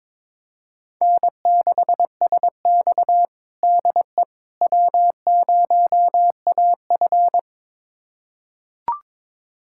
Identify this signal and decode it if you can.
Morse code, heard as a keyed tone, about 22 wpm, 700 Hz. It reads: N6SX DE W0AF
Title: USA Callsign DE Callsign - 4 Characters - 22wpm